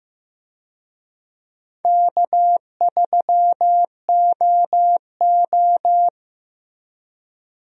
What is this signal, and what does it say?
Morse code, heard as a keyed tone, about 15 wpm, 700 Hz. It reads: K3OO
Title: USA Call Signs - 4 Characters - Repeated 3x - 15wpm